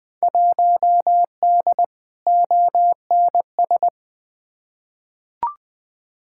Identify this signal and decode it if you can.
Morse code, heard as a keyed tone, about 20 wpm, 700 Hz. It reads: 1D ONS
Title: Field Day Exchanges - 20wpm